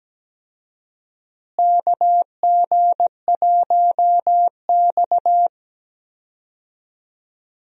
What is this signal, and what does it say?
Morse code, heard as a keyed tone, about 17 wpm, 700 Hz. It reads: KG1X
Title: USA Call Signs - 4 Characters - Repeated 2x - 17wpm